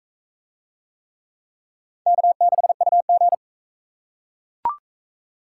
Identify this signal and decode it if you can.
Morse code, heard as a keyed tone, about 36 wpm, 700 Hz. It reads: K6UG